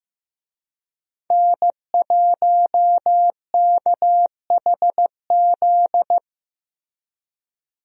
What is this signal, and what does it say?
Morse code, heard as a keyed tone, about 15 wpm, 700 Hz. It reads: N1KHZ
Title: USA Call Signs - 5 Characters - Repeated 2x - 15wpm